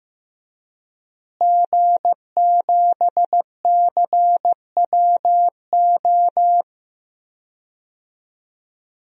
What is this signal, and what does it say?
Morse code, heard as a keyed tone, about 15 wpm, 700 Hz. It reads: G7CWO
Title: DXCC-100 Call Signs - 5 Characters - Repeated 2x - 15wpm